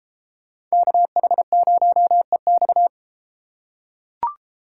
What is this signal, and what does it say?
Morse code, heard as a keyed tone, about 33 wpm, 700 Hz. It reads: KH0EX